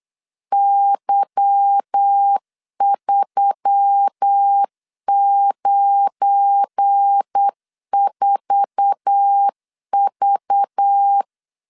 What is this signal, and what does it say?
Morse code, heard as a keyed tone, about 8 wpm, 800 Hz. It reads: Y394V
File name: LEZIONI / 8 YJ45 / 8 YJ4 8wpm 3246